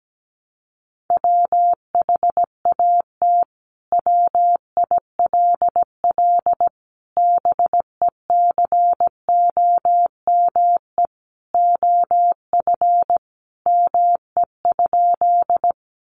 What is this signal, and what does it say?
Morse code, heard as a keyed tone, about 17 wpm, 700 Hz. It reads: WHAT WILL BECOME OF ME?
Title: Sentences from Top 600 Words 17wpm